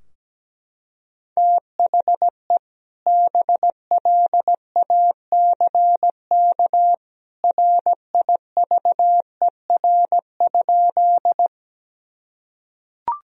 Morse code, heard as a keyed tone, about 17 wpm, 700 Hz. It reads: THE BLACK RIVER?
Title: Sentences from Top 1000 Words - Review Part 2 - 17wpm